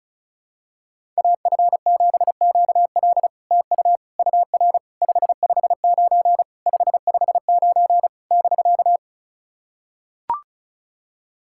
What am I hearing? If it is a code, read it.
Morse code, heard as a keyed tone, about 34 wpm, 700 Hz. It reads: AF7QL TU UR 559 559 <BK>
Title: SOTA QSO Practice 35wpm